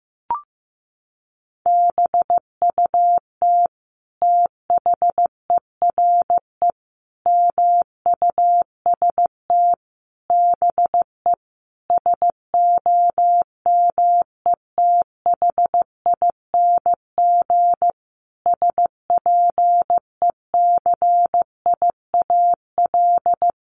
Morse code, heard as a keyed tone, about 15 wpm, 700 Hz. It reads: BUT THERE MUST BE SOMETHING SPECIAL
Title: Sentences from Top 1500 Words 15wpm